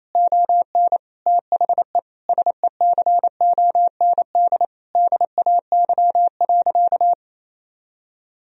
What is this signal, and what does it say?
Morse code, heard as a keyed tone, about 28 wpm, 700 Hz. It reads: ON THE SECOND DAY.
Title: Sentences from Top 1000 Words - Review Part 4 - 28wpm